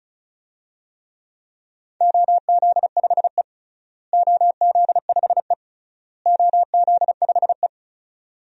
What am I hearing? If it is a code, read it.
Morse code, heard as a keyed tone, about 34 wpm, 700 Hz. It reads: OZ5E OZ5E OZ5E
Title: DXCC-100 Call Signs - 4 Characters - Repeated 3x - 35wpm